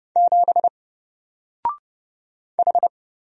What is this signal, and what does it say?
Morse code, heard as a keyed tone, about 30 wpm, 700 Hz. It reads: 7 H
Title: Single Letter-Number Alt 30wpm